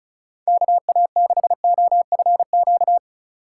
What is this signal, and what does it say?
Morse code, heard as a keyed tone, about 45 wpm, 700 Hz. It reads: KA6OFQ